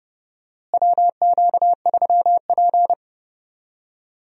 Morse code, heard as a keyed tone, about 30 wpm, 700 Hz. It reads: WQ3P
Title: USA Call Signs - 4 Characters - Repeated 3x - 30wpm